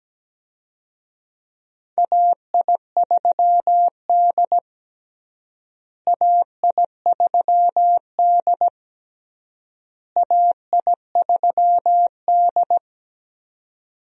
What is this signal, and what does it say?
Morse code, heard as a keyed tone, about 17 wpm, 700 Hz. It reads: AI3D AI3D AI3D
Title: USA Call Signs - 4 Characters - Repeated 3x - 17wpm